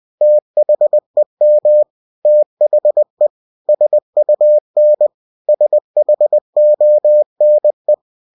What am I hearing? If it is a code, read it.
Morse code, heard as a keyed tone, about 20 wpm, 600 Hz. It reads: THEM THE SUN SHONE
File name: cw-books / winnie-the-pooh / winnie-20wpm-page-0028